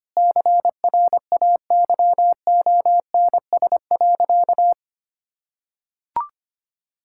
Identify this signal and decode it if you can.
Morse code, heard as a keyed tone, about 25 wpm, 700 Hz. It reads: CRAYONS.
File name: Easy Sentences with Hints - 25wpm